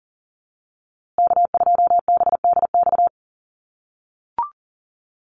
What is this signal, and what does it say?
Morse code, heard as a keyed tone, about 40 wpm, 700 Hz. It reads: K2BDX
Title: USA Call Signs Repeated 2x 40wpm